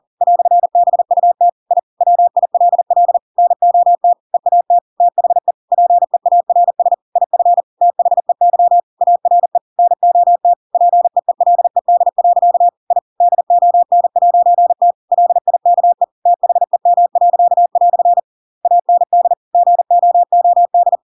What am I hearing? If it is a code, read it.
Morse code, heard as a keyed tone, about 40 wpm, 700 Hz. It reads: "BUT I WILL NOT EAT THE PEARS IF THEY ARE NOT PEELED. I DON'T LIKE THEM." AND GOOD
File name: Pinocchio / Pinocchio ch7 40 40 ik1hge